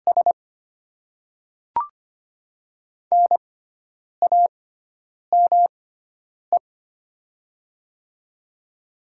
Morse code, heard as a keyed tone, about 25 wpm, spaced as for 9 wpm, 700 Hz. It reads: S NAME
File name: Top 20 QSO Elements - 6x Character Spacing - 25wpm